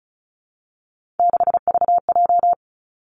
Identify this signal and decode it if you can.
Morse code, heard as a keyed tone, about 36 wpm, 700 Hz. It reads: 6VJ
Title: Three Challenging Letter-Number Combinations 35wpm